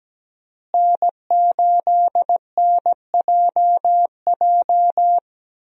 Morse code, heard as a keyed tone, about 17 wpm, 700 Hz. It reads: N8NJJ
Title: USA Call Signs - 5 Characters - Repeated 2x - 17wpm